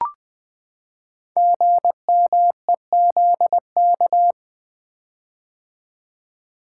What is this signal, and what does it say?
Morse code, heard as a keyed tone, about 20 wpm, 700 Hz. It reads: GMEZK
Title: Code Groups - 5 Letter:Numbers - 20wpm